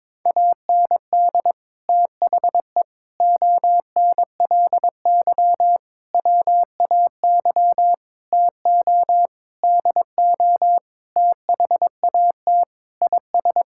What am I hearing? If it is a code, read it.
Morse code, heard as a keyed tone, about 22 wpm, 700 Hz. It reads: AND THE ONLY WAY TO DO THAT IS
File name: Sets of 9 Words - 22wpm